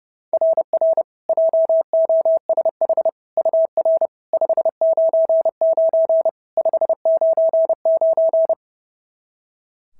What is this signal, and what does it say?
Morse code, heard as a keyed tone, about 30 wpm, 690 Hz. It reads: RR JOSH UR 599 599
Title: SOTA QSO Practice 30wpm